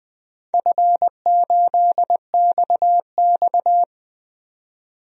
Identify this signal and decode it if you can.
Morse code, heard as a keyed tone, about 20 wpm, 700 Hz. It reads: F8XX
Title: DXCC-100 Call Signs - 4 Characters - 20wpm